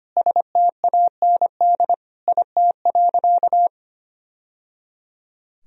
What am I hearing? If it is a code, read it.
Morse code, heard as a keyed tone, about 25 wpm, 700 Hz. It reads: STAND IT.
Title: Sentences from Top 1000 Words - Review Part 1 - 25wpm